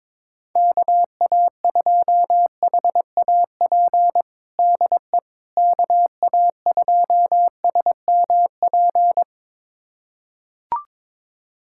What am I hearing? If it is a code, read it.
Morse code, heard as a keyed tone, about 22 wpm, 700 Hz. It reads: KA2HAP DE KA2SMP
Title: USA Callsign DE Callsign - 6 Characters - 22wpm